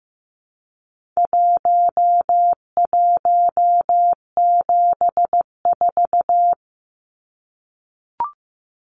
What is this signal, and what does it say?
Morse code, heard as a keyed tone, about 15 wpm, 700 Hz. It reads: 1174
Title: Numbers - Contest Style - 4-Digit 15wpm